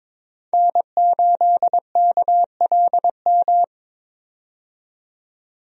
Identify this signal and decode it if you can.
Morse code, heard as a keyed tone, about 22 wpm, 700 Hz. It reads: N8KLM
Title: USA Call Signs - 5 Characters - 22wpm